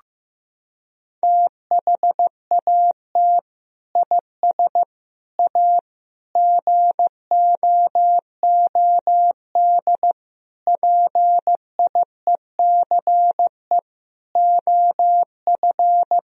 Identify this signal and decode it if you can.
Morse code, heard as a keyed tone, about 15 wpm, 700 Hz. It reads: THAT IS A GOOD PIECE OF
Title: Sentences from Top 1000 Words - Review Part 3 - 15wpm